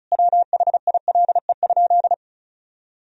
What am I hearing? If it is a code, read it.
Morse code, heard as a keyed tone, about 34 wpm, 700 Hz. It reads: WHILE?
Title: Sentences from Top 300 Words and Abbreviations 35wpm